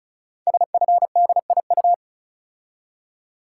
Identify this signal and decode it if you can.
Morse code, heard as a keyed tone, about 36 wpm, 700 Hz. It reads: SFDIU
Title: Code Groups - 5 Letters - 35wpm